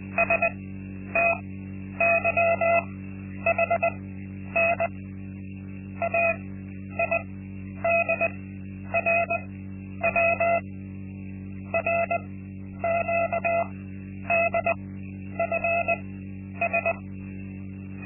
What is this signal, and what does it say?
Morse code, heard as a keyed tone, about 20 wpm, spaced as for 10 wpm, 680 Hz. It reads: STYHN AIDRW RQDFS
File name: Linksmakalnis-2